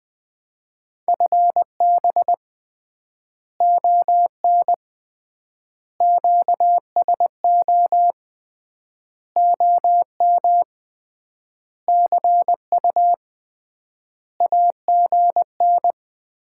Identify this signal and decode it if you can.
Morse code, heard as a keyed tone, about 20 wpm, 700 Hz. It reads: FB ON QSO OM CU AGN
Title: Common QSO Phrases - 3x Word-Spacing - 20wpm